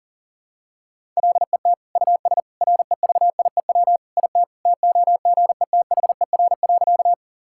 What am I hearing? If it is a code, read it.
Morse code, heard as a keyed tone, about 40 wpm, 700 Hz. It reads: LET US REVIEW IT TOGETHER.
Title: Sentences from Top 1000 Words - Review Part 6 - 40wpm